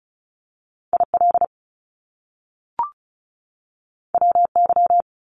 Morse code, heard as a keyed tone, about 36 wpm, 700 Hz. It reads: IL WY
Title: US States 35wpm